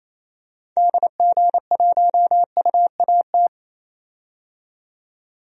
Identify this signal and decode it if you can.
Morse code, heard as a keyed tone, about 27 wpm, 700 Hz. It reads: DG1UAT
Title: DXCC-100 Call Signs - 6 Characters - 28wpm